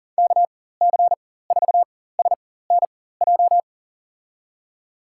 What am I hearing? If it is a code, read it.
Morse code, heard as a keyed tone, about 40 wpm, spaced as for 19 wpm, 700 Hz. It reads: KC4SNJ
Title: USA Call Signs - 6 Characters - 4x Character Spacing - 40wpm